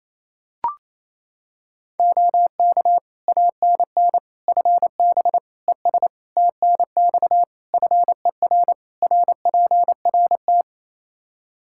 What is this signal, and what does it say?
Morse code, heard as a keyed tone, about 28 wpm, 700 Hz. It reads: OK ANN FB ES TNX FER RPRT
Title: K1USN SST QSO Practice 28wpm